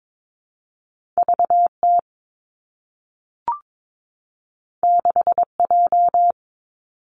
Morse code, heard as a keyed tone, about 22 wpm, 700 Hz. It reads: V T 6 J
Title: Two Challenging Letter-Number Combinations 22wpm